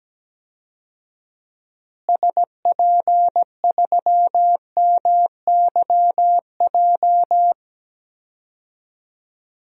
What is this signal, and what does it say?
Morse code, heard as a keyed tone, about 17 wpm, 700 Hz. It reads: SP3MYJ
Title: DXCC-100 Call Signs - 6 Characters - Repeated 3x - 17wpm